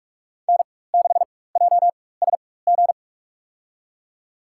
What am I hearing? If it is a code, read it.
Morse code, heard as a keyed tone, about 46 wpm, spaced as for 21 wpm, 700 Hz. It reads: N6JSG
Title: USA Call Signs - 4x Character Spacing - 45wpm